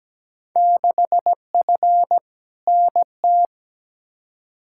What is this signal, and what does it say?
Morse code, heard as a keyed tone, about 17 wpm, 700 Hz. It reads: 6F NT
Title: Field Day Exchanges Encore - 17wpm